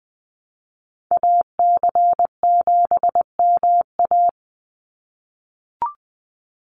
Morse code, heard as a keyed tone, about 20 wpm, 700 Hz. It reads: AC7MA
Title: NA SOTA Activator Callsigns 20wpm